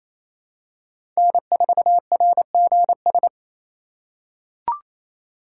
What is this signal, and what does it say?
Morse code, heard as a keyed tone, about 28 wpm, 700 Hz. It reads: N4RGS
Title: USA Call Signs - 5 Characters - 4x Character Spacing - 28wpm